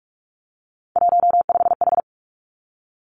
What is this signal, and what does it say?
Morse code, heard as a keyed tone, about 46 wpm, 700 Hz. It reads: 15H
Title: Three Challenging Letter-Number Combinations 45wpm